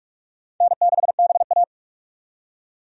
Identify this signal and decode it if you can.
Morse code, heard as a keyed tone, about 47 wpm, 700 Hz. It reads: N6BA
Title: USA Call Signs - 4 Characters - 45wpm